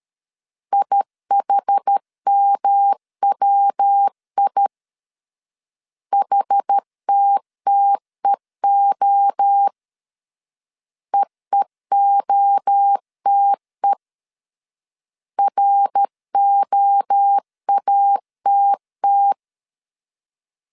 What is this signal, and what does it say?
Morse code, heard as a keyed tone, about 13 wpm, 800 Hz. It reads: IHMWI HTTEO EEOTE ROATT
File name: CW / DET / 1 3 1500C S C 12wpm 2750